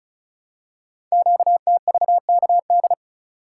Morse code, heard as a keyed tone, about 36 wpm, 700 Hz. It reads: QTVKD
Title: Code Groups - 5 Letter:Numbers - 35wpm